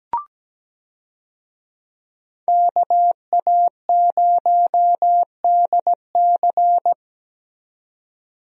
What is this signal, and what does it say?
Morse code, heard as a keyed tone, about 17 wpm, 700 Hz. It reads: KA0DC
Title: USA Call Signs - 5 Characters - 17wpm